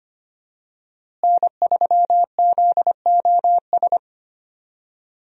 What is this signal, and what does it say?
Morse code, heard as a keyed tone, about 25 wpm, 700 Hz. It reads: N3ZOS
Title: USA Call Signs - 5 Characters - Repeated 2x - 25wpm